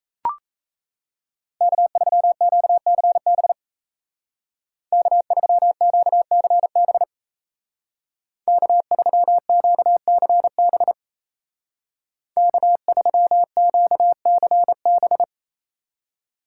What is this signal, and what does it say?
Morse code, heard as a keyed tone, about 34 wpm, 700 Hz. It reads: K3QCB K3QCB K3QCB K3QCB
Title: Speed Racing - USA Call Signs 28wpm